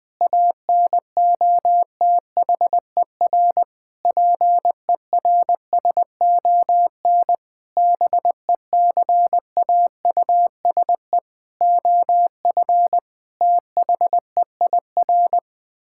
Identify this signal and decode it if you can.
Morse code, heard as a keyed tone, about 20 wpm, 700 Hz. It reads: ANOTHER PERSON BECAUSE OF THEIR
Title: Sentences with 9-Word Phrases - 20wpm